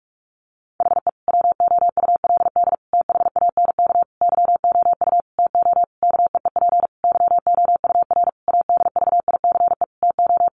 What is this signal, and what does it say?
Morse code, heard as a keyed tone, about 44 wpm, 700 Hz. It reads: HE WOULD THANK YOU TO KEEP YOUR ADVICE TO
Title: Sentences from Top 1100 Words 45wpm